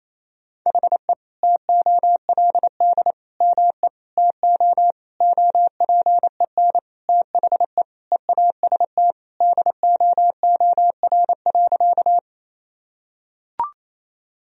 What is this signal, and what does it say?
Morse code, heard as a keyed tone, about 28 wpm, 700 Hz. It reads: HE TOLD ME TO OPEN THE EAST DOOR.
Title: Sentences from Top 1000 Words - Review Part 3 - 28wpm